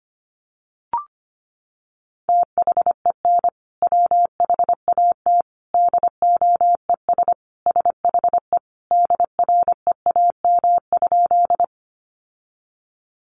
Morse code, heard as a keyed tone, about 25 wpm, 700 Hz. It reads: THEN WHAT DOES SHE DREAM?
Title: Sentences from Top 1500 Words 25wpm